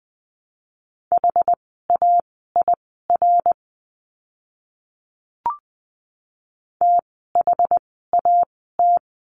Morse code, heard as a keyed tone, about 20 wpm, spaced as for 14 wpm, 700 Hz. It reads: HAIR THAT
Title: 101 Lesson 17f - Letter u - Words 20wpm